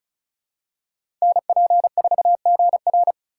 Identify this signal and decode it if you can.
Morse code, heard as a keyed tone, about 34 wpm, 700 Hz. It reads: NP4GR
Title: USA Call Signs - 2x Character Spacing - 35wpm